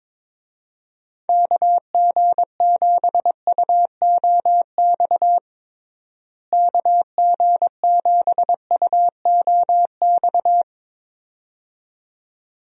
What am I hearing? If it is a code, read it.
Morse code, heard as a keyed tone, about 21 wpm, 700 Hz. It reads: KG7UOX KG7UOX
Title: USA Call Signs - 6 Characters - Repeated 2x - 22wpm